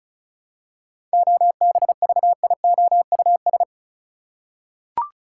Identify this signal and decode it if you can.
Morse code, heard as a keyed tone, about 34 wpm, 700 Hz. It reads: OBVIOUS
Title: Seven Letter Words - 35wpm